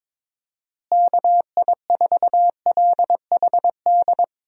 Morse code, heard as a keyed tone, about 22 wpm, 700 Hz. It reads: KI4LHD
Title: USA Call Signs - 6 Characters - Repeated 2x - 22wpm